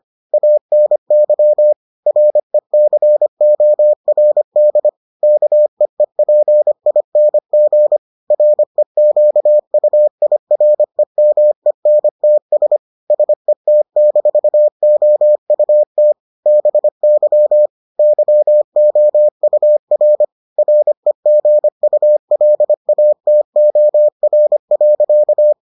Morse code, heard as a keyed tone, about 25 wpm, 600 Hz. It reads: ANY RECORD KEEPING REQUIREMENTS SET-OUT BY YOUR REGULATOR.